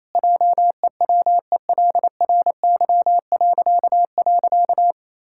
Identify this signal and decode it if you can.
Morse code, heard as a keyed tone, about 28 wpm, 700 Hz. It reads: JEWELRY..
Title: Pangram Challenge v2 - 28wpm